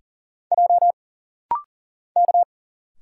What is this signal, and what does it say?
Morse code, heard as a keyed tone, about 40 wpm, 700 Hz. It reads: J K
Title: Single Letters - Rapid-Fire 40wpm